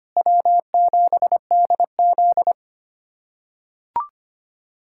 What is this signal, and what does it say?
Morse code, heard as a keyed tone, about 25 wpm, 700 Hz. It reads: W7DZ
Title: USA Callsign DE Callsign - 4 Characters - 25wpm